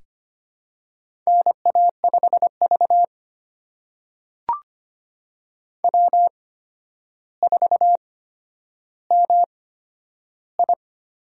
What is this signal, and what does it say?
Morse code, heard as a keyed tone, about 25 wpm, spaced as for 11 wpm, 700 Hz. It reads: NA5V W4MI
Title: USA Call Signs - 4 Characters - 8x Character Spacing - 25wpm